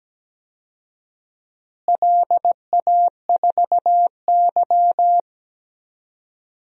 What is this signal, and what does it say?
Morse code, heard as a keyed tone, about 17 wpm, 700 Hz. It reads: LA4Y
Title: DXCC-100 Call Signs - 4 Characters - Repeated 3x - 17wpm